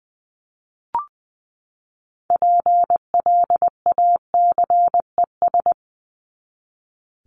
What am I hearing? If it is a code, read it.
Morse code, heard as a keyed tone, about 20 wpm, 700 Hz. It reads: PLACES
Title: ES:S Suffix - Top 100 Words with Base Words 20wpm